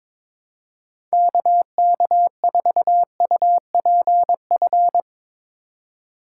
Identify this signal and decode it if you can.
Morse code, heard as a keyed tone, about 21 wpm, 700 Hz. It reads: KK4UPF
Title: USA Call Signs - 6 Characters - 6x Character Spacing - 22wpm